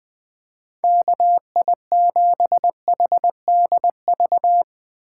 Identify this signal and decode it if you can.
Morse code, heard as a keyed tone, about 20 wpm, 700 Hz. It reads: KI7HDV
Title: USA Call Signs - 6 Characters - 2x Character Spacing - 20wpm